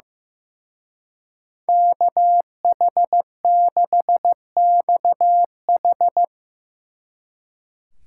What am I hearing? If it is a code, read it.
Morse code, heard as a keyed tone, about 17 wpm, 700 Hz. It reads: KH6XH
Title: Speed Racing - USA 2x2 Call Signs 15wpm